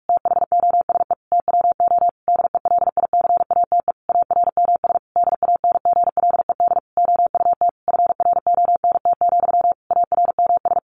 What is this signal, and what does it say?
Morse code, heard as a keyed tone, about 44 wpm, 700 Hz. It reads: THOSE TWO DELICATE ARMS DANGLED OUT FRONT, ARMS
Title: A Sound Of Thunder 45wpm